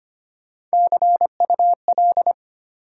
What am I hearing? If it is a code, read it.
Morse code, heard as a keyed tone, about 25 wpm, 700 Hz. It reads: CUL
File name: Top 90 QSO Elements - 2x Character Spacing - 25wpm